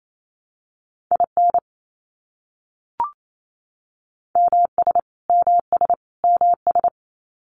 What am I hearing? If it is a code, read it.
Morse code, heard as a keyed tone, about 28 wpm, 700 Hz. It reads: IN MS MS MS
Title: US States Repeated 3x 28wpm